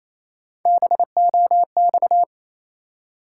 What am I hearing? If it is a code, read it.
Morse code, heard as a keyed tone, about 28 wpm, 700 Hz. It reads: BOX
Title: Three Letter Words - 28wpm